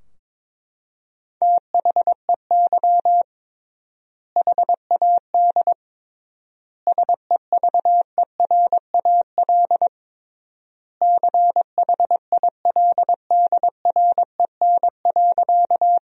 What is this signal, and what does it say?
Morse code, heard as a keyed tone, about 22 wpm, 700 Hz. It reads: THEY HAD SEVERAL CHILDREN.
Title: 3x Word-Spacing - Sentences from Top 400 Words - 22wpm